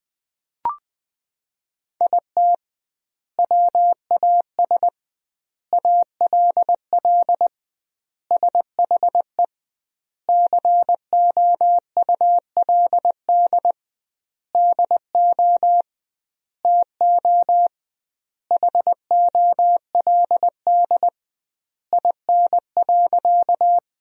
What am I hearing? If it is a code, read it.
Morse code, heard as a keyed tone, about 21 wpm, 700 Hz. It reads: IT WAS ALL SHE COULD DO TO HOLD IN.
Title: 2x Word-Spacing - Sentences from Top 400 Words - 20wpm